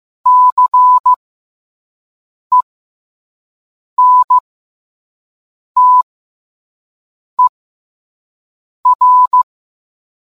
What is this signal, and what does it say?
Morse code, heard as a keyed tone, about 15 wpm, spaced as for 5 wpm, 1000 Hz. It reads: CENTER